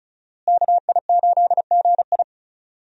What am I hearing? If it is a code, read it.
Morse code, heard as a keyed tone, about 36 wpm, 700 Hz. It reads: KI8GI
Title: USA Call Signs - 5 Characters - Repeated 2x - 35wpm